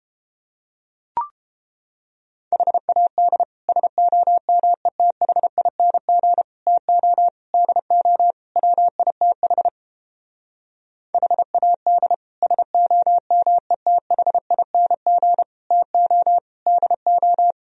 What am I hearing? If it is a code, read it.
Morse code, heard as a keyed tone, about 29 wpm, 700 Hz. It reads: HAD SOMETHING TO DO WITH HAD SOMETHING TO DO